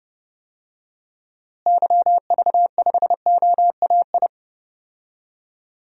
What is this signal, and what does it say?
Morse code, heard as a keyed tone, about 32 wpm, 700 Hz. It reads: YV5OAI
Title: DXCC-100 Call Signs - 6 Characters - Repeated 3x - 30wpm